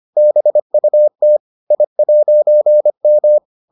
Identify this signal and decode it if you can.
Morse code, heard as a keyed tone, about 25 wpm, 600 Hz. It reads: BUT I'M